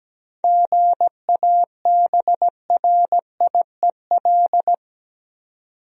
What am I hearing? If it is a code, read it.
Morse code, heard as a keyed tone, about 17 wpm, 700 Hz. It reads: GABRIEL
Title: USA Common Names - 17wpm